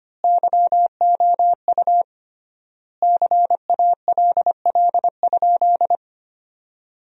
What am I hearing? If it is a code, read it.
Morse code, heard as a keyed tone, about 25 wpm, 700 Hz. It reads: YOU CALL?
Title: 3x Word-Spacing - Sentences from Top 100 Words - 25wpm